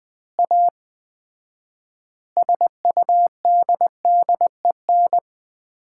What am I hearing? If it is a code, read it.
Morse code, heard as a keyed tone, about 20 wpm, 700 Hz. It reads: A SUDDEN
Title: 4x Word-Spacing - Sets of 4 Words - 20wpm